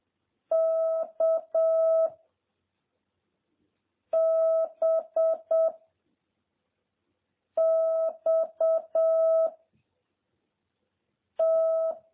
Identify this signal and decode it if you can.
Morse code, heard as a keyed tone, about 7 wpm, 650 Hz. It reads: K B X T